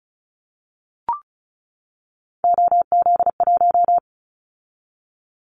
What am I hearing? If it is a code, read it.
Morse code, heard as a keyed tone, about 36 wpm, 700 Hz. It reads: OZ1
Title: Callsign Prefixes for Top 100 DXCC - 35wpm